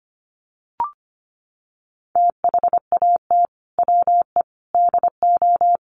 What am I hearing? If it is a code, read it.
Morse code, heard as a keyed tone, about 25 wpm, 700 Hz. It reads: THAT WE DO
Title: Sets of 5 Words Encore 25wpm